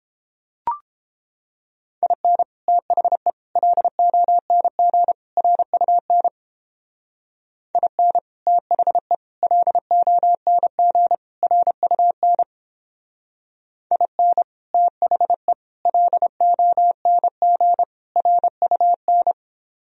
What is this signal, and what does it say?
Morse code, heard as a keyed tone, about 31 wpm, 700 Hz. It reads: IN THE LONG RUN IN THE LONG RUN IN THE LONG RUN